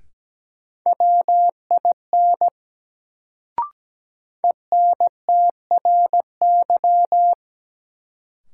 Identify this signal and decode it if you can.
Morse code, heard as a keyed tone, about 17 wpm, 700 Hz. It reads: WIN ENTRY
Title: Sentences from Top 1500 Words - New Word Drill - 17wpm